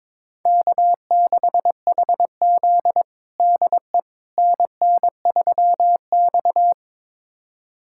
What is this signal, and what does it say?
Morse code, heard as a keyed tone, about 21 wpm, 700 Hz. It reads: K6HZ DE NN3X
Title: USA Callsign DE Callsign - 4 Characters - 22wpm